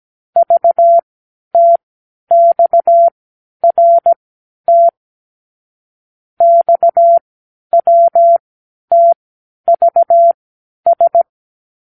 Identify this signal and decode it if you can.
Morse code, heard as a keyed tone, about 17 wpm, spaced as for 10 wpm, 680 Hz. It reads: VTXRT XWTVS
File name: k7qo / Lesson 36 (A-X)